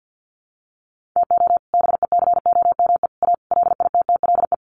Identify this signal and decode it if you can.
Morse code, heard as a keyed tone, about 50 wpm, 700 Hz. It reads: TO BECOME A LITTLE
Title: Sentences from Top 1300 Words 50wpm